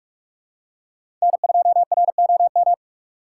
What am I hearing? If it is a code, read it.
Morse code, heard as a keyed tone, about 44 wpm, 700 Hz. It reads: N2ROM